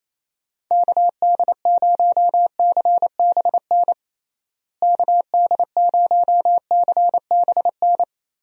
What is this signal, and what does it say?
Morse code, heard as a keyed tone, about 28 wpm, 700 Hz. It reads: KD0CBN KD0CBN
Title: USA Call Signs - 6 Characters - Repeated 3x - 28wpm